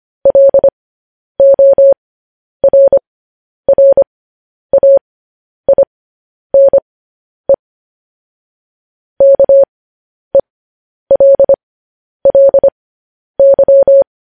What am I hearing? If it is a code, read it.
Morse code, heard as a keyed tone, about 25 wpm, spaced as for 10 wpm, 550 Hz. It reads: LORRAINE KELLY
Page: News Headlines in Morse Code at 10 WPM